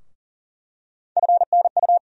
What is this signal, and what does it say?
Morse code, heard as a keyed tone, about 40 wpm, 700 Hz. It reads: FNU
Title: USA Call Sign Postfixes - 3 Letters 40wpm